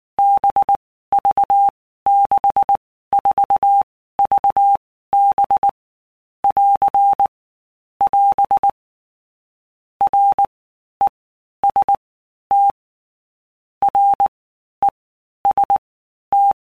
Morse code, heard as a keyed tone, about 19 wpm, 800 Hz. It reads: B V 6 4 V B <AR> <AS> R E S T R E S T
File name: Basic Letter Groups By Blocks / Groups@18 / VB46@18